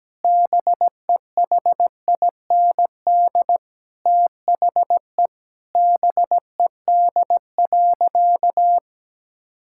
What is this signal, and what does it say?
Morse code, heard as a keyed tone, about 17 wpm, 700 Hz. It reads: BEHIND THE BED.
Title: Sentences from Top 1000 Words - Review Part 5 - 17wpm